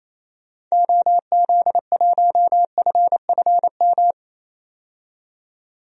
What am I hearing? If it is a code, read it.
Morse code, heard as a keyed tone, about 28 wpm, 700 Hz. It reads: OZ1FFM
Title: DXCC-100 Call Signs - 6 Characters - 28wpm